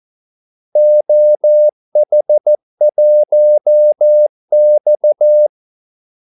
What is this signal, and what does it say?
Morse code, heard as a keyed tone, about 14 wpm, 600 Hz. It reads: OH1X
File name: Callsigns-14wpm